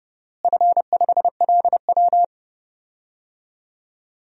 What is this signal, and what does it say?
Morse code, heard as a keyed tone, about 32 wpm, 700 Hz. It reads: F5LW
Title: DXCC-100 Call Signs - 4 Characters - Repeated 3x - 30wpm